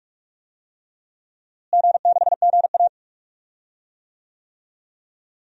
Morse code, heard as a keyed tone, about 42 wpm, 700 Hz. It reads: G6GA
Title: DXCC-100 Call Signs - 4 Characters - Repeated 3x - 45wpm